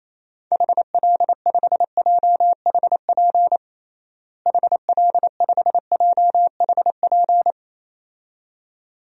Morse code, heard as a keyed tone, about 28 wpm, 700 Hz. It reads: HL5JHP HL5JHP
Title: DXCC-100 Call Signs - 6 Characters - Repeated 2x - 28wpm